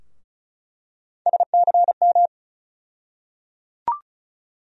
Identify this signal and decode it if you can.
Morse code, heard as a keyed tone, about 34 wpm, 700 Hz. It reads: SCM
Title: USA Call Sign Postfixes - 3 Letters 35wpm